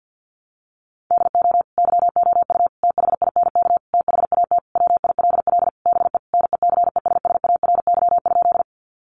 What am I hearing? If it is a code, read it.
Morse code, heard as a keyed tone, about 50 wpm, 700 Hz. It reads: DO YOU THINK THAT WILL BE NECESSARY?
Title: Sentences from Top 800 Words 50wpm